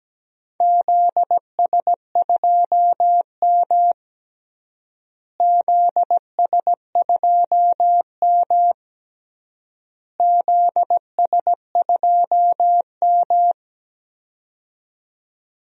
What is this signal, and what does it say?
Morse code, heard as a keyed tone, about 17 wpm, 700 Hz. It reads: ZS2M ZS2M ZS2M
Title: DXCC-100 Call Signs - 4 Characters - Repeated 3x - 17wpm